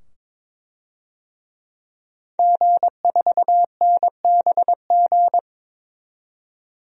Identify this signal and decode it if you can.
Morse code, heard as a keyed tone, about 22 wpm, 700 Hz. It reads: G4NBG
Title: DXCC-100 Call Signs - 5 Characters - 22wpm